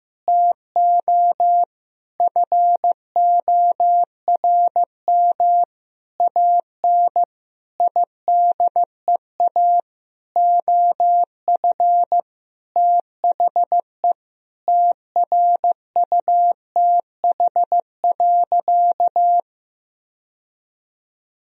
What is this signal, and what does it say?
Morse code, heard as a keyed tone, about 15 wpm, 700 Hz. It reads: TO FORM AN IDEA OF THE TRUTH.
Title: Sentences from Top 1000 Words - Review Part 1 - 15wpm